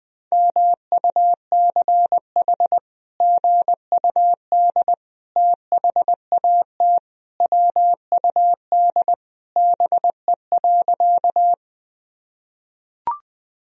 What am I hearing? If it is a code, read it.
Morse code, heard as a keyed tone, about 19 wpm, 700 Hz. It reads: MUCH GUD THAT WUD BE.
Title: Sentences from Top 200 Words and Abbreviations 20wpm